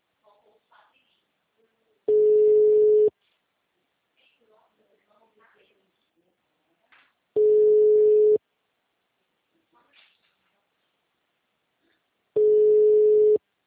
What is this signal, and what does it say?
Ringback tone (Europe, 425 Hz), with speech.